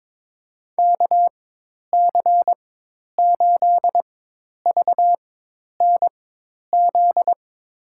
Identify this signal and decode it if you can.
Morse code, heard as a keyed tone, about 22 wpm, spaced as for 10 wpm, 700 Hz. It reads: KC8VNZ
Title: USA Call Signs - 6 Characters - 4x Character Spacing - 22wpm